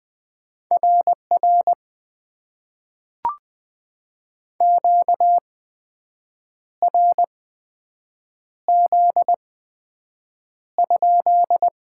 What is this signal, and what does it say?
Morse code, heard as a keyed tone, about 20 wpm, spaced as for 7 wpm, 700 Hz. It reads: RR QRZ?
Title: Top 90 QSO Elements - 8x Character Spacing - 20wpm